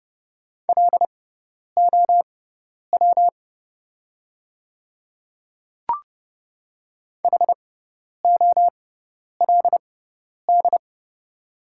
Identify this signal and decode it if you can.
Morse code, heard as a keyed tone, about 30 wpm, spaced as for 10 wpm, 700 Hz. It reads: LOW HOLD
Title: 6x Character Spacing - Top 500 Words - 30wpm